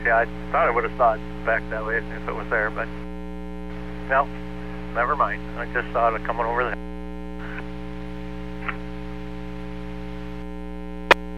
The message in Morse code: none

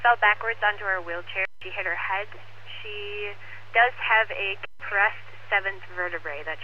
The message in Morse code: none

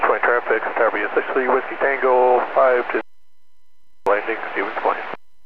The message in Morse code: none

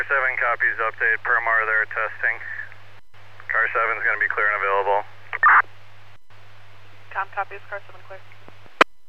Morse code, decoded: none

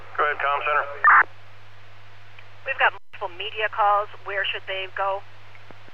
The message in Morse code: none

